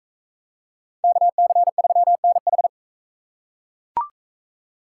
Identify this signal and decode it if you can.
Morse code, heard as a keyed tone, about 44 wpm, 700 Hz. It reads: KC3NH